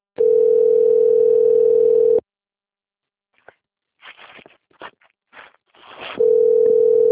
Ringback tone (North America), 440 + 480 Hz.
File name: rg-1600-unknown- - - .11075